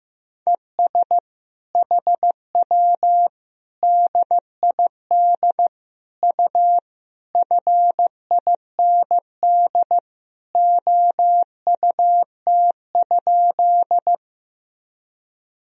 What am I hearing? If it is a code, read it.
Morse code, heard as a keyed tone, about 15 wpm, 700 Hz. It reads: ES HW DID U FIND OUT?
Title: Sentences from Top 100 Words and Abbreviations 15wpm